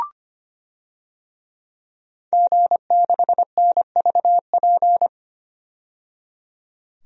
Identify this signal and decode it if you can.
Morse code, heard as a keyed tone, about 25 wpm, 700 Hz. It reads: G6NVP
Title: DXCC-100 Call Signs - 5 Characters - 25wpm